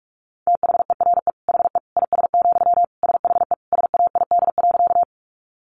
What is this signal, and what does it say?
Morse code, heard as a keyed tone, about 44 wpm, 700 Hz. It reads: THERE HE IS, SHE SAID.
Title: Sentences from Top 100 Words 45wpm